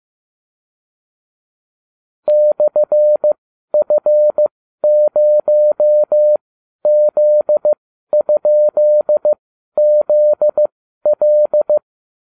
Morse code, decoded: /F0Z?ZL

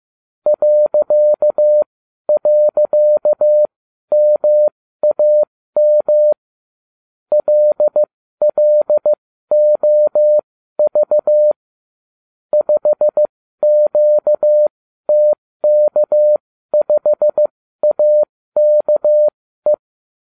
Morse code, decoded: ..MAM LLOV 5QTK5AKE

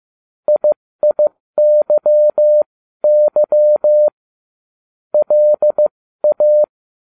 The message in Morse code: IIYY LA

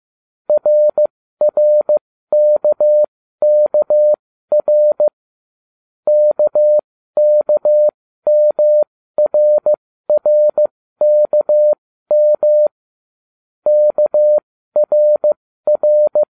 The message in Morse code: RRKKR KKMRRKM KRR

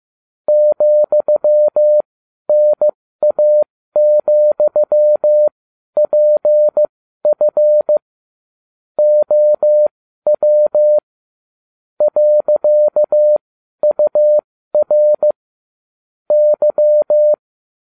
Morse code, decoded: ,NA,PF OW .UR Y

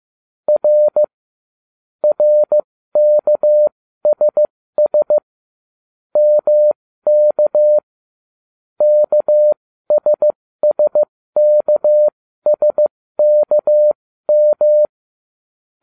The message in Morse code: R RKSS MK KSSKSKM

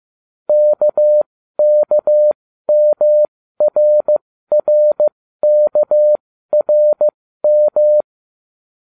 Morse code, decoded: KKMRRKRM

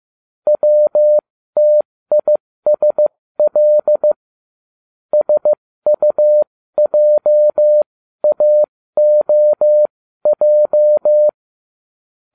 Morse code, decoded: WTISL SUJAOJ

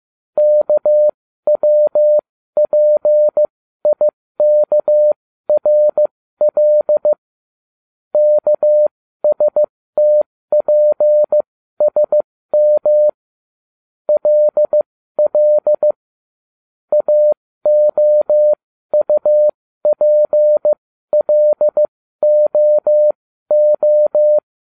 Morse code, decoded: KWPIKRL KSTPSM LL AOUPLOO

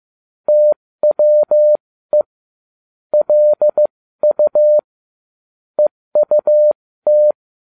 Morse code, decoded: TWE LU EUT

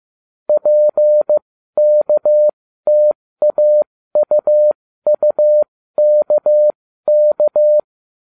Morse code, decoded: PKTAUUKK

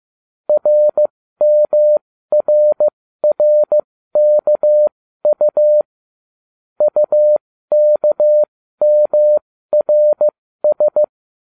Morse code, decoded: RMRRKU UKMRS

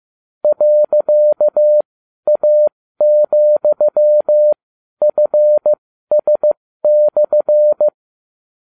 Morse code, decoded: .A,FS/

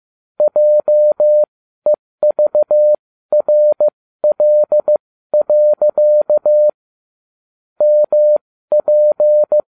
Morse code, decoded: JEVRL. MP

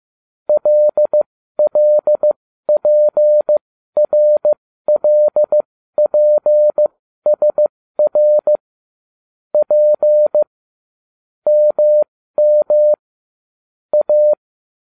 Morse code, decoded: LLPRLPSR P MM A

